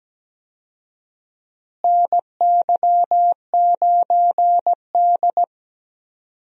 Morse code, heard as a keyed tone, about 17 wpm, 700 Hz. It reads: NY9D
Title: USA Call Signs - 4 Characters - Repeated 2x - 17wpm